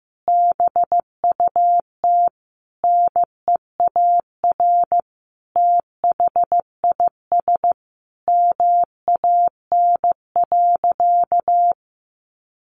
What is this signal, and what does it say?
Morse code, heard as a keyed tone, about 15 wpm, 700 Hz. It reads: BUT NEAR THIS MAN.
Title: Sentences from Top 1400 Words 15wpm